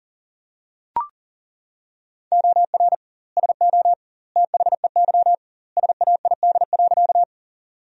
Morse code, heard as a keyed tone, about 40 wpm, 700 Hz. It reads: OR SO THEY SAID.
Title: 2x Word-Spacing - Sentences from Top 100 Words - 40wpm